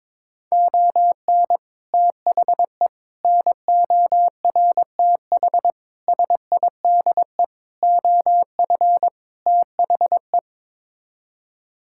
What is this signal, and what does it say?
Morse code, heard as a keyed tone, about 21 wpm, 700 Hz. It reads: ON THE NORTH SIDE OF THE
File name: practice / 22wpm / Sets of 6 Words - 22wpm